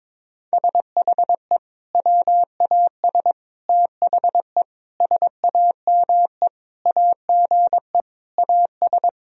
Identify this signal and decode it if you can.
Morse code, heard as a keyed tone, about 22 wpm, 700 Hz. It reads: SHE WAS THE SAME AGE AS
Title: Sentences from Top 1000 Words - Review Part 3 - 22wpm